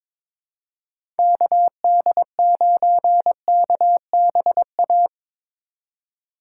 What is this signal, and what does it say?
Morse code, heard as a keyed tone, about 22 wpm, 700 Hz. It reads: KD9KBA
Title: USA Call Signs - 4x Character Spacing - 22wpm